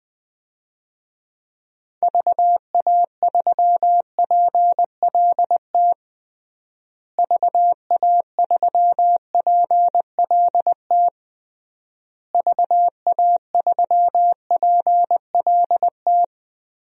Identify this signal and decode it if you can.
Morse code, heard as a keyed tone, about 21 wpm, 700 Hz. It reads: VA3PLT VA3PLT VA3PLT
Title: DXCC-100 Call Signs - 6 Characters - Repeated 3x - 20wpm